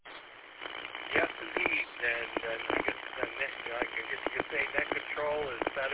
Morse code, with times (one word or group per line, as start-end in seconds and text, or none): none